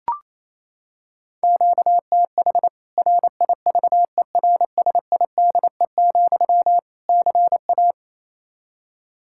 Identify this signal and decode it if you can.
Morse code, heard as a keyed tone, about 28 wpm, 700 Hz. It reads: QTH RIVERSIDE, CA
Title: Practice QSO - Exchange 1 - Beginning - 28wpm